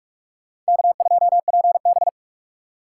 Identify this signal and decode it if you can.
Morse code, heard as a keyed tone, about 58 wpm, 700 Hz. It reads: K2PB